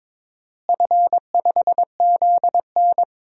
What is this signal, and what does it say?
Morse code, heard as a keyed tone, about 22 wpm, 700 Hz. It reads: F5ZN
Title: DXCC-100 Call Signs - 4 Characters - Repeated 3x - 22wpm